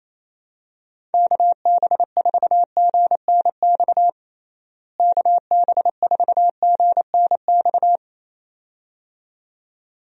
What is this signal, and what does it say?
Morse code, heard as a keyed tone, about 27 wpm, 700 Hz. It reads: KB4GNX KB4GNX
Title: USA Call Signs - 6 Characters - Repeated 2x - 28wpm